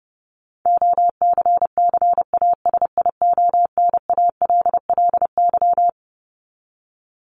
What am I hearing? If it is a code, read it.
Morse code, heard as a keyed tone, about 30 wpm, 700 Hz. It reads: OCCASIONALLY
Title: LY Suffix - Top 100 Words with Base Words 30wpm